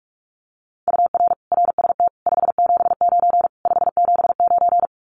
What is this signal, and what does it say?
Morse code, heard as a keyed tone, about 46 wpm, 700 Hz. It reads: UR RST 579 579
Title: Common QSO Phrases Encore 45wpm